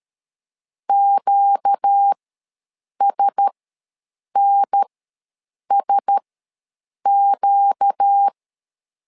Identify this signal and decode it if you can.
Morse code, heard as a keyed tone, about 13 wpm, spaced as for 7 wpm, 800 Hz. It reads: QSNSQ
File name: LEZIONI / 4 Q1S ws 12wpm 1827